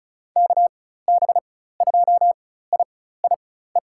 Morse code, heard as a keyed tone, about 34 wpm, spaced as for 16 wpm, 700 Hz. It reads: KB2IIE
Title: USA Call Signs - 6 Characters - 4x Character Spacing - 35wpm